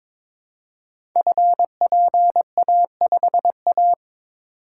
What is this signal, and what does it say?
Morse code, heard as a keyed tone, about 22 wpm, 700 Hz. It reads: FPA5A
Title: Code Groups - 5 Letter:Numbers - 22wpm